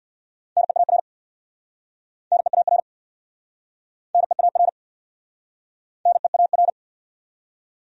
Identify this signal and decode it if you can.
Morse code, heard as a keyed tone, about 69 wpm, 700 Hz. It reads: NEAR NEAR NEAR NEAR